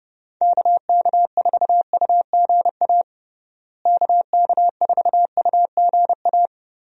Morse code, heard as a keyed tone, about 32 wpm, 700 Hz. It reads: KK4UGA KK4UGA
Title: USA Call Signs - 6 Characters - Repeated 3x - 30wpm